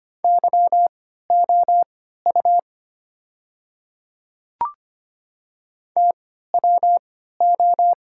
Morse code, heard as a keyed tone, about 25 wpm, spaced as for 14 wpm, 700 Hz. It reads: YOU TWO
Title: 3x Character Spacing - Top 100 Words - 25wpm